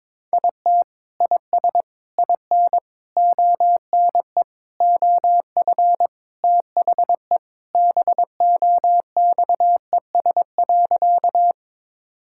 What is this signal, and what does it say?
Morse code, heard as a keyed tone, about 22 wpm, 700 Hz. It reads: IT IS IN ONE OF THE BOXES.
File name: practice / 22wpm / Sentences with 3-Word Phrases - 22wpm